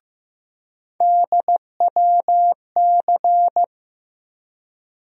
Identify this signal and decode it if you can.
Morse code, heard as a keyed tone, about 15 wpm, 700 Hz. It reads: DWC
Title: USA Call Sign Postfixes - 3 Letters 15wpm